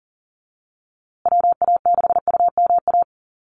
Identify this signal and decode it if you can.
Morse code, heard as a keyed tone, about 40 wpm, 700 Hz. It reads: WA6UMA
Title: USA Call Signs Repeated 3x 40wpm